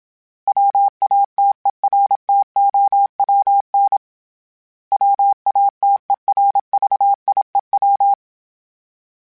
Morse code, heard as a keyed tone, about 27 wpm, 800 Hz. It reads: WATERTOWN WATERVIEW
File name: fromSMs / FromSMs mp3s / Cities@25